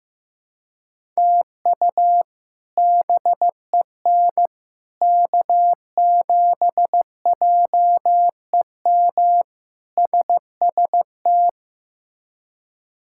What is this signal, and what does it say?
Morse code, heard as a keyed tone, about 15 wpm, 700 Hz. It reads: TU BEN K7JEM SST
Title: K1USN SST QSO Practice - 15wpm